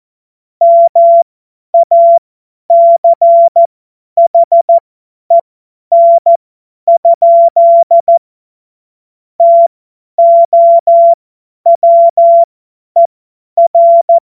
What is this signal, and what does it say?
Morse code, heard as a keyed tone, about 14 wpm, spaced as for 10 wpm, 680 Hz. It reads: MACHEN? TOWER
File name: Morsekurs / Witze / Witz 25. Juli 2023 50 F70